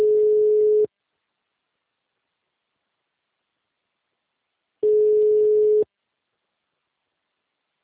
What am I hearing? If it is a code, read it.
Ringback tone (Europe, 425 Hz).